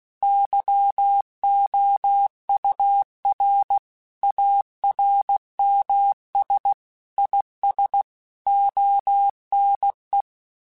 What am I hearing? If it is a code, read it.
Morse code, heard as a keyed tone, about 16 wpm, 800 Hz. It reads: YOUR ARMS IS ONE
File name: Quotes / Quotes-mp3s / ChildAsleep @15